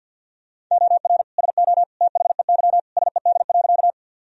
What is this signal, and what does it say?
Morse code, heard as a keyed tone, about 47 wpm, 700 Hz. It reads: OR SO THEY SED.
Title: Sentences from Top 100 Words and Abbreviations 50wpm